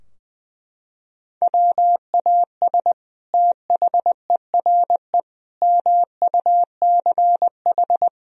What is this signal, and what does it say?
Morse code, heard as a keyed tone, about 20 wpm, 700 Hz. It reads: WAS THERE MUCH
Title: Sentences from Top 1000 Words - Review Part 4 - 20wpm